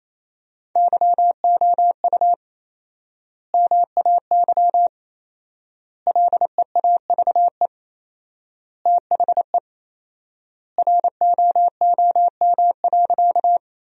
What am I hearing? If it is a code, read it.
Morse code, heard as a keyed tone, about 28 wpm, 700 Hz. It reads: YOU MAY LEAVE THE ROOM.
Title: 4x Word-Spacing - Sentences from Top 400 Words - 28wpm